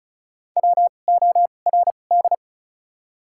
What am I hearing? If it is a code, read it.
Morse code, heard as a keyed tone, about 34 wpm, spaced as for 25 wpm, 700 Hz. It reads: WORD
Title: 2x Character Spacing - Top 200 Words - 35wpm